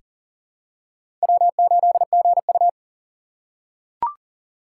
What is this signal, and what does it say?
Morse code, heard as a keyed tone, about 40 wpm, 700 Hz. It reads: W8GU
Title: K1USN SST QSO Practice - 40wpm